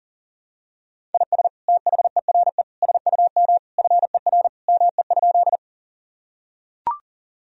Morse code, heard as a keyed tone, about 40 wpm, 700 Hz. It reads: IS THERE SUM FER ME?
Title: Sentences from Top 200 Words and Abbreviations 40wpm